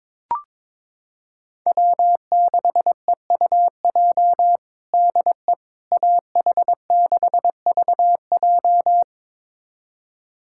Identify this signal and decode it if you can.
Morse code, heard as a keyed tone, about 22 wpm, 700 Hz. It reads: W6EUJ DE AH6VJ
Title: USA Callsign DE Callsign - 5 Characters - 22wpm